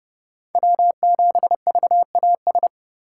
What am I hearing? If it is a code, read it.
Morse code, heard as a keyed tone, about 30 wpm, 700 Hz. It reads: W7VAS
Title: K1USN SST QSO Practice - 30wpm